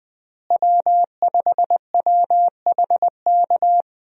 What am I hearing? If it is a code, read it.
Morse code, heard as a keyed tone, about 21 wpm, 700 Hz. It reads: W5WHK
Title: USA Call Signs - 5 Characters - Repeated 3x - 20wpm